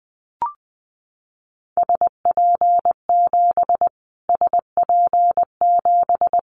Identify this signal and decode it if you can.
Morse code, heard as a keyed tone, about 20 wpm, 700 Hz. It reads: SP7 SP7
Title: Callsign Prefixes for Top 25 DXCC - Repeated 3x - 20wpm